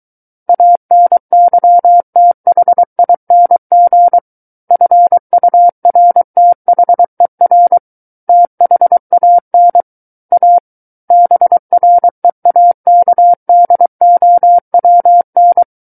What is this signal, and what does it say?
Morse code, heard as a keyed tone, about 23 wpm, 700 Hz. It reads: ANYTHING FURTHER THAN A BREAKDOWN
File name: Book One / 14-Chapter 14 - 23 WPM